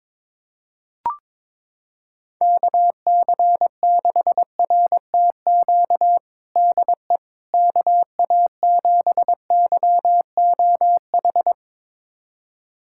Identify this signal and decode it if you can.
Morse code, heard as a keyed tone, about 22 wpm, 700 Hz. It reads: KC6RTQ DE KA7YOH
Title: USA Callsign DE Callsign - 6 Characters - 22wpm